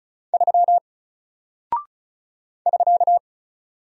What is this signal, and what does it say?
Morse code, heard as a keyed tone, about 36 wpm, 700 Hz. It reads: 3 <SK>
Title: Single Character (Letter:Number:Punctuation:Prosign) 35wpm